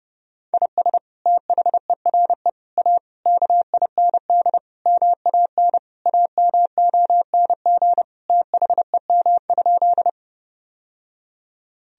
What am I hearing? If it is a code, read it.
Morse code, heard as a keyed tone, about 32 wpm, 700 Hz. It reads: IS THERE A KIND MAN AMONG THEM?
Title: Sentences from Top 1000 Words - Review Part 3 - 30wpm